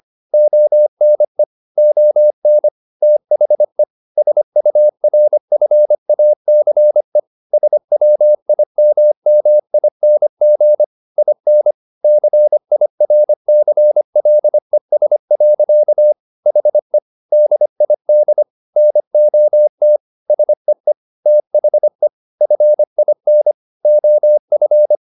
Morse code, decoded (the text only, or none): ONE ON THE SURFACE SWIMMING IN CIRCLES. HE DID NOT SEE THE FIN OF